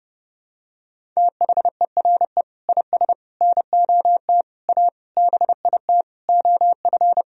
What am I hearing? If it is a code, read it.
Morse code, heard as a keyed tone, about 30 wpm, 700 Hz. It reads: THERE IS NOT A BIT OF
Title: Sentences from Top 1000 Words - Review Part 1 - 30wpm